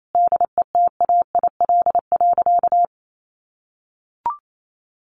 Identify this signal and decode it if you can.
Morse code, heard as a keyed tone, about 28 wpm, 700 Hz. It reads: DETAIL.
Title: Sentences from Top 1500 Words 28wpm